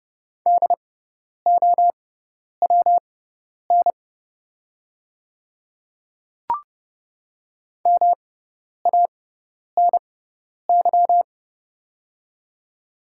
Morse code, heard as a keyed tone, about 30 wpm, spaced as for 10 wpm, 690 Hz. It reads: DOWN MANY
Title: 6x Character Spacing - Top 100 Words - 30wpm